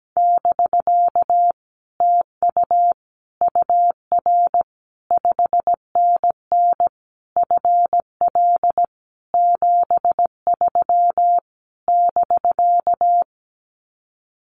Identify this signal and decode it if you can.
Morse code, heard as a keyed tone, about 17 wpm, 700 Hz. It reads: <BK> TU UR 5NN FL 73 <BK>
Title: Common QSO Phrases Encore 17wpm